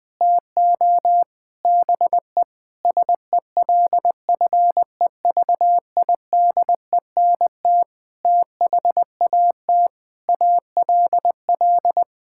Morse code, heard as a keyed tone, about 20 wpm, 700 Hz. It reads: TO BE SELFEVIDENT THAT ALL
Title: Sets of 9 Words - 20wpm